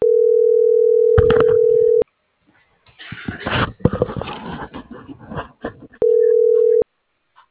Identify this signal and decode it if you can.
Ringback tone (North America), 440 + 480 Hz.